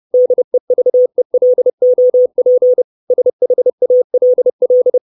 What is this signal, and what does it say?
Morse code, heard as a keyed tone, about 30 wpm, 500 Hz. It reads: DEVELOP SHALL
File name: 632 Words 30WPM